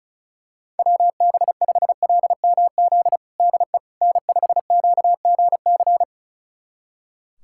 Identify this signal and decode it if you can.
Morse code, heard as a keyed tone, about 36 wpm, 700 Hz. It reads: WB5LMZ DE N5QGC
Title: USA Callsign DE Callsign 35wpm